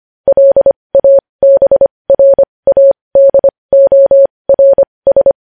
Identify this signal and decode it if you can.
Morse code, heard as a keyed tone, about 25 wpm, spaced as for 20 wpm, 550 Hz. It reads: LABRADORS
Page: News Headlines in Morse Code at 20 WPM